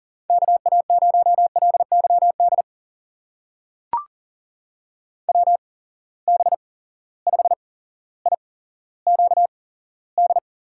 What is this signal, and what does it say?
Morse code, heard as a keyed tone, about 42 wpm, spaced as for 17 wpm, 700 Hz. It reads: KA0LYD WB5IQD
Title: USA Call Signs - 6 Characters - 8x Character Spacing - 40wpm